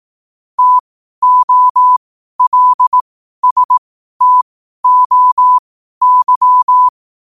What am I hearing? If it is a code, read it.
Morse code, heard as a keyed tone, about 18 wpm, spaced as for 12 wpm, 1000 Hz. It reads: TOLSTOY
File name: qotdfiles / 12wpm